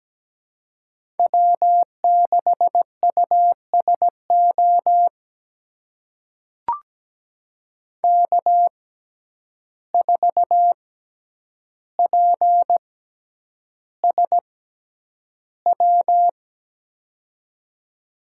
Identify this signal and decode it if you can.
Morse code, heard as a keyed tone, about 17 wpm, spaced as for 9 wpm, 700 Hz. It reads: W6USO K4PSW
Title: USA Call Signs - 5 Characters - 6x Character Spacing - 17wpm